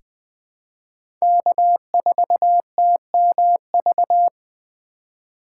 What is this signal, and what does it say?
Morse code, heard as a keyed tone, about 20 wpm, 700 Hz. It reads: K4TMV
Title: USA Call Signs - 5 Characters - 3x Character Spacing - 20wpm